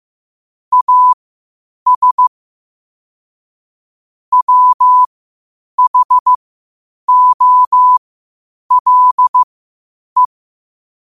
Morse code, heard as a keyed tone, about 15 wpm, spaced as for 8 wpm, 1000 Hz. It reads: AS WHOLE
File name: qotdfiles / 8wpm